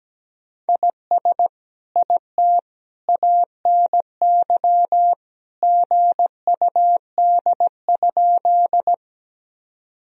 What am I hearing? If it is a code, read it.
Morse code, heard as a keyed tone, about 17 wpm, 700 Hz. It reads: IS IT ANY GUD?
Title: Sentences from Top 200 Words and Abbreviations 17wpm